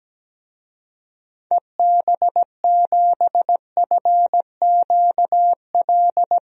Morse code, heard as a keyed tone, about 17 wpm, 700 Hz. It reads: EB7FQL
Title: DXCC-100 Call Signs - 6 Characters - 17wpm